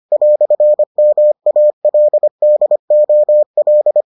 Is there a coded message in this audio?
Yes: Morse code from 0.1 to 4.0 s, heard as a keyed tone, about 25 wpm, 600 Hz. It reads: "MALDOL